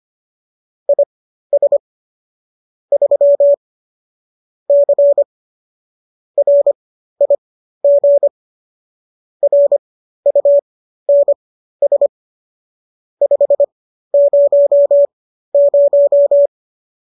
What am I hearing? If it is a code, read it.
Morse code, heard as a keyed tone, about 25 wpm, spaced as for 13 wpm, 570 Hz. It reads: IS 3 C RIG RUNS 500